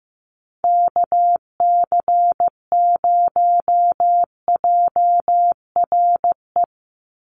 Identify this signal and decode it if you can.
Morse code, heard as a keyed tone, about 15 wpm, 700 Hz. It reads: KC0JRE
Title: USA Call Signs 15wpm